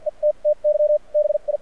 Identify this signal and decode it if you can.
Morse code, heard as a keyed tone, about 53 wpm, spaced as for 39 wpm, 600 Hz. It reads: ETTY BI
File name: archived / CRSnet